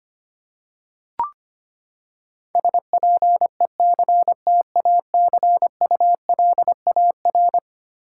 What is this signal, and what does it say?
Morse code, heard as a keyed tone, about 25 wpm, 700 Hz. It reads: SPECTACULAR
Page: morse-code-ninja-practice-25wpm